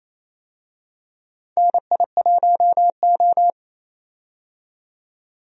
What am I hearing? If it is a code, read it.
Morse code, heard as a keyed tone, about 28 wpm, 700 Hz. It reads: NI1O
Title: USA Call Signs - 4 Characters - 28wpm